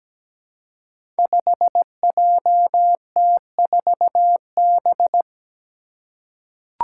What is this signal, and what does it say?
Morse code, heard as a keyed tone, about 17 wpm, 700 Hz. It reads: 5JT4B
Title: Five Challenging Letter-Number Combinations 17wpm